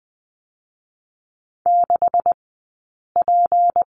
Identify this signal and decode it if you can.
Morse code, heard as a keyed tone, about 20 wpm, 700 Hz. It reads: 6 P
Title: 158 Lesson 26d - Letter g - Two Characters 20wpm